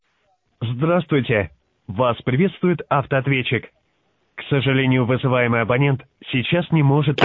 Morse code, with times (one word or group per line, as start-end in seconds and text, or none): none